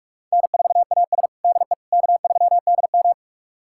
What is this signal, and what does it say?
Morse code, heard as a keyed tone, about 46 wpm, 700 Hz. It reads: N4AS DE K3DM
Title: USA Callsign DE Callsign - 4 Characters - 45wpm